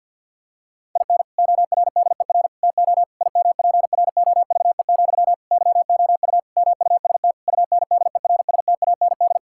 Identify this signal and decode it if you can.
Morse code, heard as a keyed tone, about 50 wpm, 700 Hz. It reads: IN ORDER TO IMPROVE, YOU MUST UNDERSTAND
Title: Sentences from Top 1000 Words - Review Part 5 - 50wpm